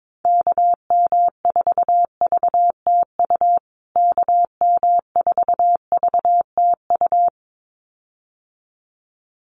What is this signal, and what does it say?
Morse code, heard as a keyed tone, about 22 wpm, 700 Hz. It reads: KM4VTU KM4VTU
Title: USA Call Signs Repeated 2x 22wpm